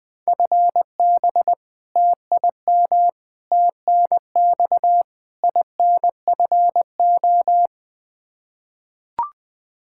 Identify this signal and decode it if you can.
Morse code, heard as a keyed tone, about 20 wpm, 700 Hz. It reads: FB TIM TNX INFO
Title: K1USN SST QSO Practice 20wpm